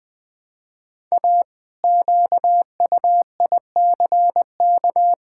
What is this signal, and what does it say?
Morse code, heard as a keyed tone, about 20 wpm, 700 Hz. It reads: A QUICK
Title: Pangram Challenge v2 - 20wpm